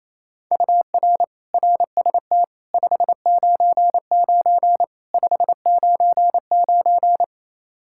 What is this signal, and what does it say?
Morse code, heard as a keyed tone, about 28 wpm, 700 Hz. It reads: UR RST 599 599
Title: Practice QSO - Exchange 1 - Beginning - 28wpm